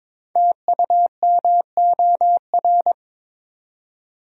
Morse code, heard as a keyed tone, about 23 wpm, 700 Hz. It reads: TUMOR
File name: 5 Letter Words - First 3 Letters Given - 22wpm